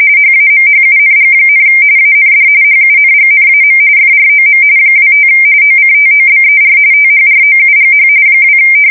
RTTY, about 45.45 baud. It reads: HAMHUNG, AND HUNGNAM FROM 25 OCTOBER THROUGH 2
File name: fsk170 / USS-Missouri